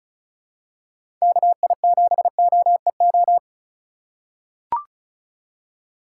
Morse code, heard as a keyed tone, about 36 wpm, 700 Hz. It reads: KI7OEO
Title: USA Call Signs - 6 Characters - 4x Character Spacing - 35wpm